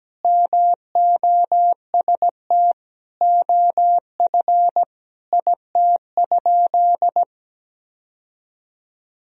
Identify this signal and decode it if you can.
Morse code, heard as a keyed tone, about 17 wpm, 700 Hz. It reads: MOST OF IT?
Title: Sentences from Top 1000 Words - Review Part 1 - 17wpm